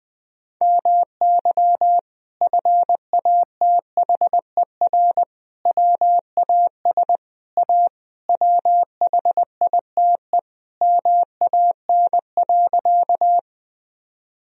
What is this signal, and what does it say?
Morse code, heard as a keyed tone, about 20 wpm, 700 Hz. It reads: MY FATHER WAS A WHITE MAN.
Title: Sentences from Top 1000 Words - Review Part 1 - 20wpm